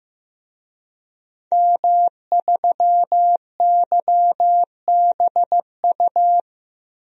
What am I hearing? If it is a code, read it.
Morse code, heard as a keyed tone, about 15 wpm, 700 Hz. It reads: M3YBU
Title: DXCC-100 Call Signs - 5 Characters - 15wpm